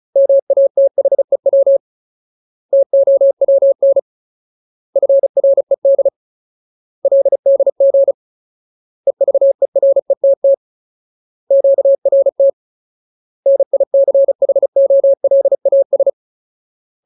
Morse code, heard as a keyed tone, about 34 wpm, 550 Hz. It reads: MATHEW TOWN FRED LDG EVERETT QRT NICHOLAS
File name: Ultimate Word List / 35WPM Ultimate